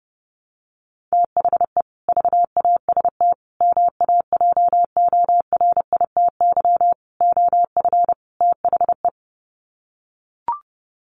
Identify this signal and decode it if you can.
Morse code, heard as a keyed tone, about 30 wpm, 700 Hz. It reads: THE VAST MAJORITY OF THE
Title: Sets of 5 Words Encore 30wpm